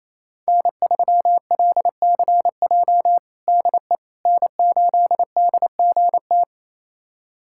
Morse code, heard as a keyed tone, about 28 wpm, 700 Hz. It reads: N3LCJ DE N8DGT
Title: USA Callsign DE Callsign 28wpm